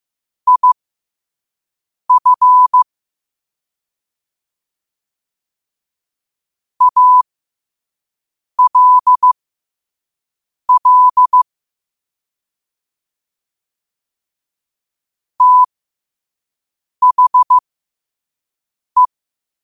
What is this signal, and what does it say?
Morse code, heard as a keyed tone, about 15 wpm, spaced as for 5 wpm, 1000 Hz. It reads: IF ALL THE
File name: qotdfiles / 5wpm